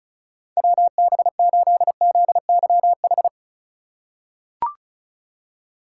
Morse code, heard as a keyed tone, about 36 wpm, 700 Hz. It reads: WB8ZYH
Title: USA Call Signs - 6 Characters - 4x Character Spacing - 35wpm